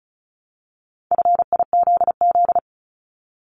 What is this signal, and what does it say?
Morse code, heard as a keyed tone, about 36 wpm, 700 Hz. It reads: FIZZ
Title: Multi-letter Phonemes 35wpm